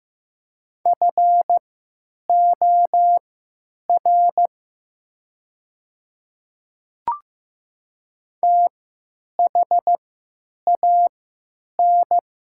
Morse code, heard as a keyed tone, about 15 wpm, spaced as for 9 wpm, 700 Hz. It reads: FOR THAN
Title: 3x Character Spacing - Top 200 Words - 15wpm